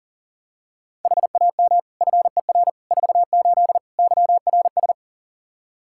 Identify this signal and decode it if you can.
Morse code, heard as a keyed tone, about 40 wpm, 700 Hz. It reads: HAM FER 48 YRS
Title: Practice QSO - Exchange 3 - More Info - 40wpm